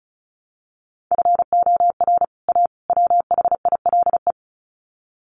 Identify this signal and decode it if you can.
Morse code, heard as a keyed tone, about 34 wpm, 700 Hz. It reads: FOR A WHILE
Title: Sets of 3 Words 35wpm